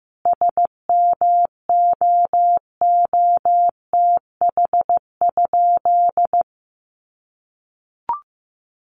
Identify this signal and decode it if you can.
Morse code, heard as a keyed tone, about 15 wpm, 700 Hz. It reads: SMOOTH?
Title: Sentences from Top 1400 Words 15wpm